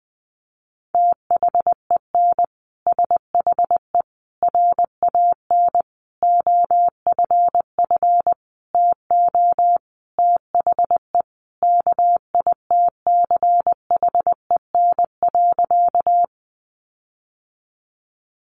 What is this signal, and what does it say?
Morse code, heard as a keyed tone, about 20 wpm, 700 Hz. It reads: THEN SHE RAN OFF TO THE KITCHEN.
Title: Sentences from Top 1400 Words 20wpm